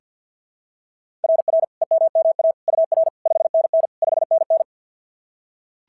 Morse code, heard as a keyed tone, about 44 wpm, 660 Hz. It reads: RR EMMA UR 5NN 5NN
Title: SOTA QSO Practice 50wpm